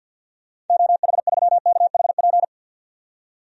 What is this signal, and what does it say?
Morse code, heard as a keyed tone, about 52 wpm, 700 Hz. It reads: OH3KHP
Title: DXCC-100 Call Signs - 6 Characters - 50wpm